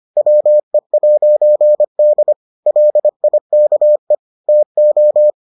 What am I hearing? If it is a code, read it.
Morse code, heard as a keyed tone, about 25 wpm, 600 Hz. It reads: WE'D LIKE TO